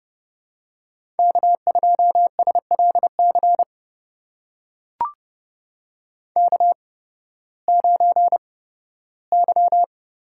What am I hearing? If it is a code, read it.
Morse code, heard as a keyed tone, about 31 wpm, spaced as for 16 wpm, 700 Hz. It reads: K2SLC K9Y
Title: USA Call Signs - 5 Characters - 8x Character Spacing - 30wpm